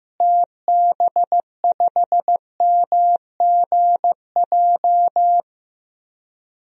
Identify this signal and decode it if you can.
Morse code, heard as a keyed tone, about 15 wpm, 700 Hz. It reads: TB5MGJ
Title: DXCC-100 Call Signs - 6 Characters - Repeated 3x - 15wpm